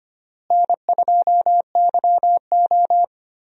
Morse code, heard as a keyed tone, about 25 wpm, 700 Hz. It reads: N2YO
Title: USA Call Signs - 4 Characters - Repeated 3x - 25wpm